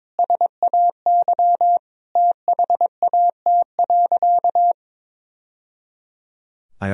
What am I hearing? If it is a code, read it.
Morse code, heard as a keyed tone, about 21 wpm, 700 Hz. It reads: SAY THAT.
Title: Sentences from Top 1000 Words - Review Part 1 - 22wpm